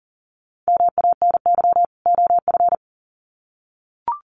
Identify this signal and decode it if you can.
Morse code, heard as a keyed tone, about 40 wpm, 700 Hz. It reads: MANY OF
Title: Sets of 2 Words 40wpm